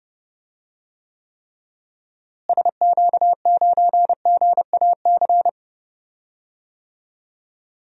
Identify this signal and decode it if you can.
Morse code, heard as a keyed tone, about 30 wpm, 700 Hz. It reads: SQ9GAC
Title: DXCC-100 Call Signs - 6 Characters - Repeated 3x - 30wpm